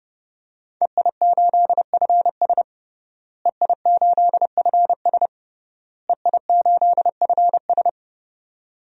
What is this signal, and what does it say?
Morse code, heard as a keyed tone, about 32 wpm, 700 Hz. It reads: EI8FS EI8FS EI8FS
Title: DXCC-100 Call Signs - 5 Characters - Repeated 3x - 30wpm